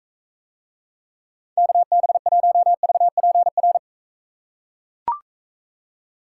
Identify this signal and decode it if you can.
Morse code, heard as a keyed tone, about 41 wpm, 700 Hz. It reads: KB1VPR